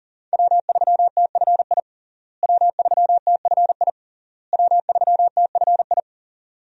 Morse code, heard as a keyed tone, about 40 wpm, 700 Hz. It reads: W3TFI W3TFI W3TFI
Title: USA Call Signs - 5 Characters - Repeated 3x - 40wpm